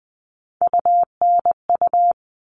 Morse code, heard as a keyed tone, about 20 wpm, 700 Hz. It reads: UNU
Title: All Cut Numbers - 3-Digit 20wpm